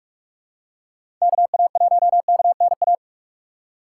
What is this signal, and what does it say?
Morse code, heard as a keyed tone, about 47 wpm, 700 Hz. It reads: KA1KNA
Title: USA Call Signs - 6 Characters - 6x Character Spacing - 45wpm